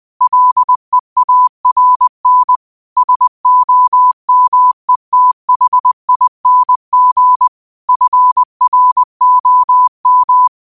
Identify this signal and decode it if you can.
Morse code, heard as a keyed tone, about 20 wpm, 1000 Hz. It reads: LEARN SOMETHING FROM